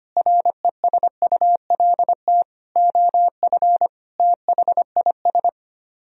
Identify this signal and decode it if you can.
Morse code, heard as a keyed tone, about 32 wpm, 700 Hz. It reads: RESULT OF THIS